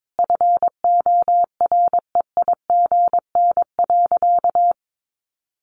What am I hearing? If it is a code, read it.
Morse code, heard as a keyed tone, about 22 wpm, 700 Hz. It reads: FOREIGN.
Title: Sentences from Top 1400 Words 22wpm